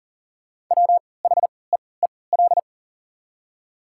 Morse code, heard as a keyed tone, about 40 wpm, spaced as for 23 wpm, 700 Hz. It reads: WHEEL
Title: 3x Character Spacing - Top 500 Words - 40wpm